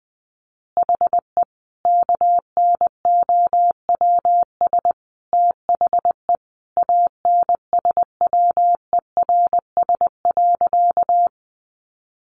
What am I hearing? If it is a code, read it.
Morse code, heard as a keyed tone, about 20 wpm, 700 Hz. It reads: HE KNOWS THE ANSWERS.
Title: Additional Easy Phrases 20wpm